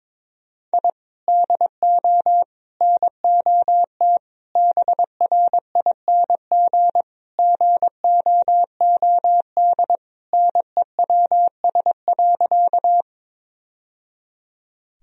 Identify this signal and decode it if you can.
Morse code, heard as a keyed tone, about 22 wpm, 700 Hz. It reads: I DO NOT BRING GOOD NEWS.
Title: Sentences from Top 1000 Words - Review Part 3 - 22wpm